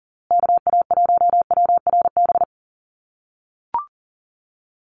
Morse code, heard as a keyed tone, about 40 wpm, 700 Hz. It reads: KA1WRB
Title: USA Call Signs Repeated 3x 40wpm